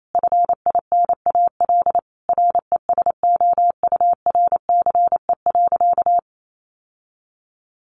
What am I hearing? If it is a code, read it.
Morse code, heard as a keyed tone, about 28 wpm, 700 Hz. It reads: FINAL RESOURCE.
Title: Sentences from Top 1300 Words 28wpm